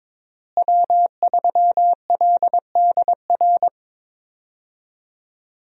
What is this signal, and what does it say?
Morse code, heard as a keyed tone, about 22 wpm, 700 Hz. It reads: W3LDR
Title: USA Call Signs - 5 Characters - Repeated 3x - 22wpm